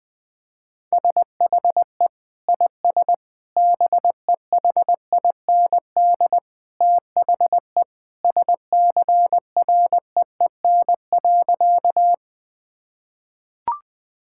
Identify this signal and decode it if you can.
Morse code, heard as a keyed tone, about 20 wpm, 700 Hz. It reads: SHE IS BEHIND THE SCREEN.
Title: Sentences from Top 1000 Words - Review Part 6 - 20wpm